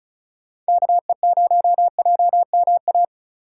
Morse code, heard as a keyed tone, about 36 wpm, 700 Hz. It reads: KE0JMA
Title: USA Call Signs - 6 Characters - 4x Character Spacing - 35wpm